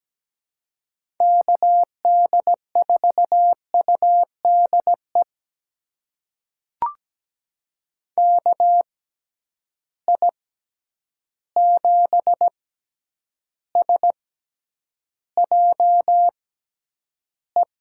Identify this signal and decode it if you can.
Morse code, heard as a keyed tone, about 17 wpm, spaced as for 9 wpm, 700 Hz. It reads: KD4UDE KI7SJE
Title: USA Call Signs - 6 Characters - 6x Character Spacing - 17wpm